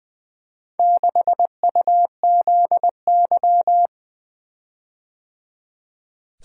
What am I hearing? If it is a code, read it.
Morse code, heard as a keyed tone, about 20 wpm, 700 Hz. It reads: 6UZY
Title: Four Letter-Number Combinations - 20wpm